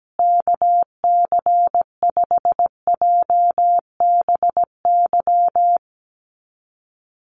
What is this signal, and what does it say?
Morse code, heard as a keyed tone, about 17 wpm, 700 Hz. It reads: KC5JBY
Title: USA Call Signs Repeated 3x 17wpm